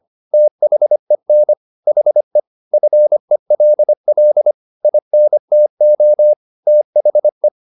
Morse code, heard as a keyed tone, about 25 wpm, 600 Hz. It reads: THEN HE FELL INTO THE